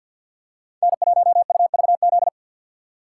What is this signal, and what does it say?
Morse code, heard as a keyed tone, about 52 wpm, 700 Hz. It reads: N1UVZ
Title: USA Call Signs - 5 Characters - 4x Character Spacing - 50wpm